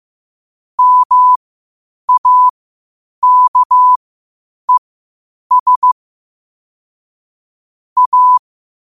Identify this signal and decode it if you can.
Morse code, heard as a keyed tone, about 15 wpm, spaced as for 8 wpm, 1000 Hz. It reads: MAKES A
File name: qotdfiles / 8wpm